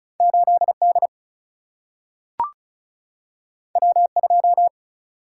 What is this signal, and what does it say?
Morse code, heard as a keyed tone, about 36 wpm, 700 Hz. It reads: 8 D W 2
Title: Single-Letter Single-Number 35wpm